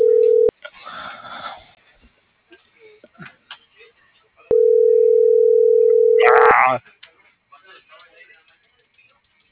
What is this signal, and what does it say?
Ringback tone (North America), 440 + 480 Hz.
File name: CENTRAL MABE / out xquenda.trinidad - .68414